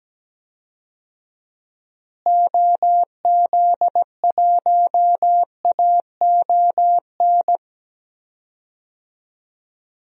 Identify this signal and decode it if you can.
Morse code, heard as a keyed tone, about 17 wpm, 700 Hz. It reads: OZ1AON
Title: DXCC-100 Call Signs - 6 Characters - Repeated 3x - 17wpm